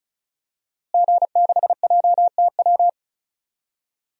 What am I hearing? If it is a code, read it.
Morse code, heard as a keyed tone, about 34 wpm, 700 Hz. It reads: G6JTW
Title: DXCC-100 Call Signs - 5 Characters - Repeated 3x - 35wpm